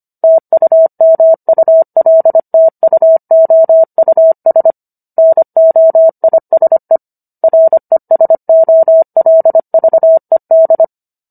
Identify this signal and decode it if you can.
Morse code, heard as a keyed tone, about 25 wpm, 650 Hz. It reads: TUMULTUOUS NOISE RESOLVED